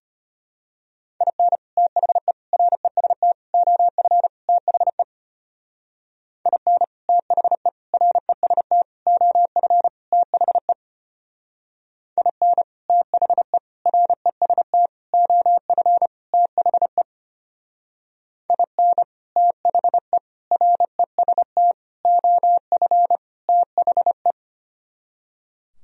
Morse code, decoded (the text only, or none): IN THE REST OF THE IN THE REST OF THE IN THE REST OF THE IN THE REST OF THE